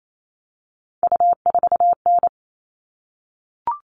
Morse code, heard as a keyed tone, about 28 wpm, 700 Hz. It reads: U4N
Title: All Cut Numbers - 3-Digit 28wpm